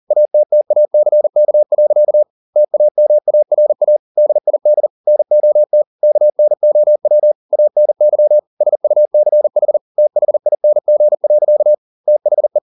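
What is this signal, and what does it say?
Morse code, heard as a keyed tone, about 40 wpm, 600 Hz. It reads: ATTACK. TAMARA DID NOT KNOW ANY SUCH THING. THE
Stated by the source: Morse code at 40 wpm, practice text, CW med Gnister.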